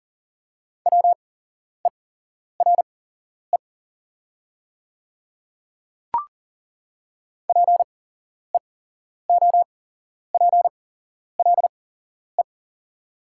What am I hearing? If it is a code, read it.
Morse code, heard as a keyed tone, about 37 wpm, spaced as for 11 wpm, 700 Hz. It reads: WERE PEOPLE
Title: 8x Character Spacing - Top 100 Words - 40wpm